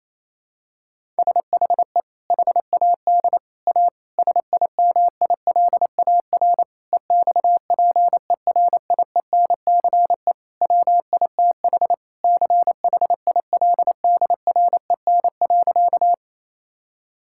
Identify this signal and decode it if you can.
Morse code, heard as a keyed tone, about 28 wpm, 700 Hz. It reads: SHE HAD A SIMILAR EXPERIENCE WITH CHILDREN.
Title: Sentences from Top 1000 Words - Review Part 5 - 28wpm